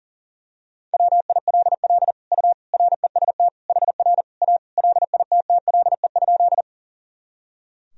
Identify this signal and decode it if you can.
Morse code, heard as a keyed tone, about 40 wpm, 700 Hz. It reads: WILL U REST HR A LITTLE?
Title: Sentences from Top 500 Words and Abbreviations 40wpm